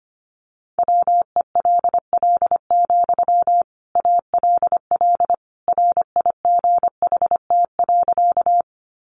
Morse code, heard as a keyed tone, about 25 wpm, 700 Hz. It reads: WELL, ALL RIGHT.
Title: Sentences from Top 200 Words 25wpm